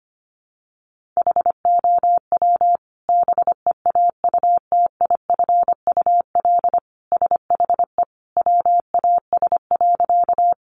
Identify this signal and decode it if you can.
Morse code, heard as a keyed tone, about 25 wpm, 700 Hz. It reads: HOW BEAUTIFUL SHE WAS.
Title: Sentences from Top 900 Words 25wpm